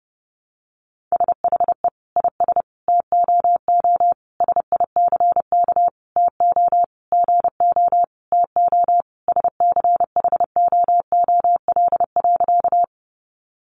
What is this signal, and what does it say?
Morse code, heard as a keyed tone, about 30 wpm, 700 Hz. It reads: SHE IS TOO SICK TO GO TO SCHOOL.
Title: Easy Phrases 30wpm